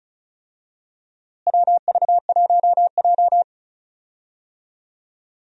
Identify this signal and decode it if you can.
Morse code, heard as a keyed tone, about 34 wpm, 700 Hz. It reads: WV1J
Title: USA Call Signs - 4 Characters - Repeated 2x - 35wpm